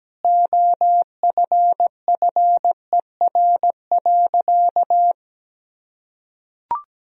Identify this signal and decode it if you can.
Morse code, heard as a keyed tone, about 17 wpm, 700 Hz. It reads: OFFER.
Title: Sentences from Top 1000 Words - Review Part 4 - 17wpm